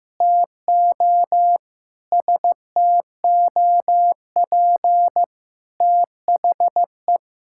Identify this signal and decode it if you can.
Morse code, heard as a keyed tone, about 15 wpm, 700 Hz. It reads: TO STOP THE
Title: Sentences from Top 1000 Words - Review Part 6 - 15wpm